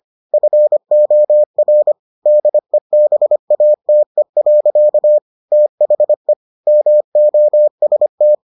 Morse code, heard as a keyed tone, about 25 wpm, 600 Hz. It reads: FOR DEBATE. THE MOST